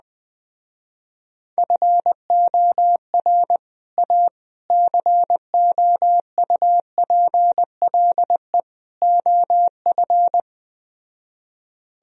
Morse code, heard as a keyed tone, about 20 wpm, 700 Hz. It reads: FOR A COUPLE OF
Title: Speed-Racing - Sets of 4 Words 17wpm